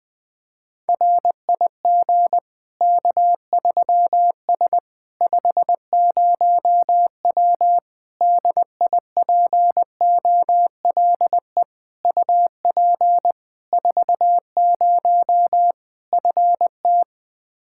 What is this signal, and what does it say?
Morse code, heard as a keyed tone, about 20 wpm, 700 Hz. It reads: RIG K3S 50W DIPOLE UP 40 FT
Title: K1USN SST QSO Practice 20wpm